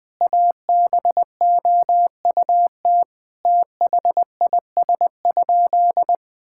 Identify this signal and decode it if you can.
Morse code, heard as a keyed tone, about 20 wpm, 700 Hz. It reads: ABOUT THIS?
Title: Sentences with 4-Word Phrases - 20wpm